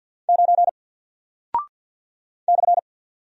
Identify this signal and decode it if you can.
Morse code, heard as a keyed tone, about 50 wpm, 700 Hz. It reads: 9 /
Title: Single Character (Letter:Number:Punctuation:Prosign) 50wpm